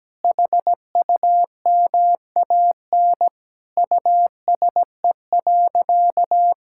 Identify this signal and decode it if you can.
Morse code, heard as a keyed tone, about 17 wpm, 700 Hz. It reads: HUMAN USE.
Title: Sentences from Top 1000 Words - Review Part 6 - 17wpm